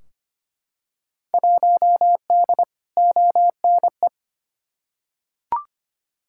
Morse code, heard as a keyed tone, about 25 wpm, 700 Hz. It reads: 1D ONE
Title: Field Day Exchanges - 25wpm